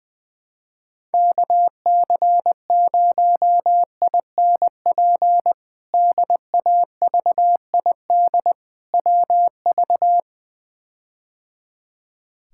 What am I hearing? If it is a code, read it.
Morse code, heard as a keyed tone, about 21 wpm, 700 Hz. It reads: KC0INP DAVID WV
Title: K1USN SST QSO Practice - 20wpm